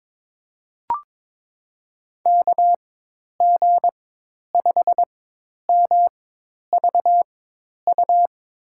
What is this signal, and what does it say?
Morse code, heard as a keyed tone, about 22 wpm, spaced as for 10 wpm, 700 Hz. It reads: KG5MVU
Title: USA Call Signs - 6 Characters - 4x Character Spacing - 22wpm